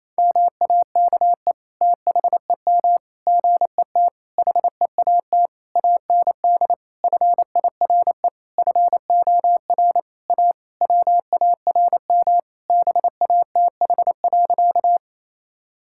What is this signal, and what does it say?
Morse code, heard as a keyed tone, about 28 wpm, 700 Hz. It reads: MAKE THEM GET HEAT AND FIRE FOR A WARM BATH.
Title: Sentences from Top 1600 Words 28wpm